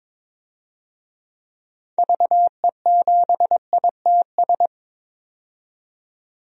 Morse code, heard as a keyed tone, about 22 wpm, 700 Hz. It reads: VE7ITS
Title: DXCC-100 Call Signs - 6 Characters - 22wpm